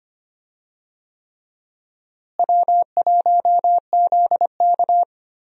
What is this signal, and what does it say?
Morse code, heard as a keyed tone, about 25 wpm, 700 Hz. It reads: W1ZK
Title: USA Call Signs - 4 Characters - Repeated 3x - 25wpm